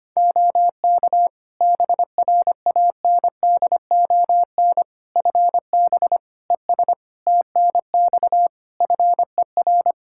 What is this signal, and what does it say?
Morse code, heard as a keyed tone, about 25 wpm, 700 Hz. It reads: OK BRANDON FB ES TNX FER
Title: Practice QSO - Exchange 2 - Info - 25wpm